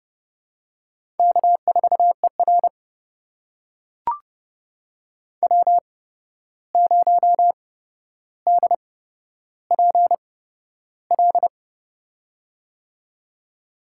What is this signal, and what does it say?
Morse code, heard as a keyed tone, about 32 wpm, spaced as for 12 wpm, 700 Hz. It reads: K4ER W0DPL
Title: USA Call Signs - 8x Character Spacing - 30wpm